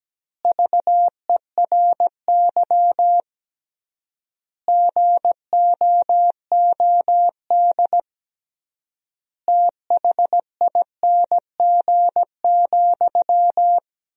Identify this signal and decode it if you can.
Morse code, heard as a keyed tone, about 17 wpm, 700 Hz. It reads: VERY GOOD THING,
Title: 3x Word-Spacing - Sentences from Top 200 Words - 17wpm